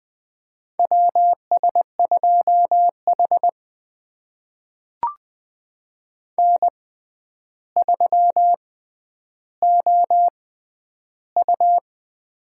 Morse code, heard as a keyed tone, about 21 wpm, spaced as for 10 wpm, 700 Hz. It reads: WS2H N3OU
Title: USA Call Signs - 6x Character Spacing - 20wpm